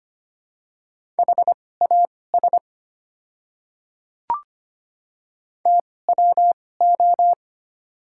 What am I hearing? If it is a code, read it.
Morse code, heard as a keyed tone, about 25 wpm, spaced as for 18 wpm, 700 Hz. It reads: HAS TWO
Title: 2x Character Spacing - Top 100 Words - 25wpm